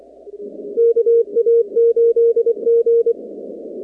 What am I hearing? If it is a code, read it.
Morse code, heard as a keyed tone, about 23 wpm, 460 Hz. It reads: KA8G
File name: contests / 2019-cw-wpx-kt5j-k5pi- -2330 0-44